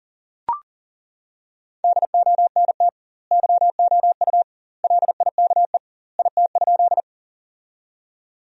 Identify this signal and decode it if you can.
Morse code, heard as a keyed tone, about 40 wpm, 700 Hz. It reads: DONT YOU LIKE IT?
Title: 2x Word-Spacing - Sentences from Top 300 Words - 40wpm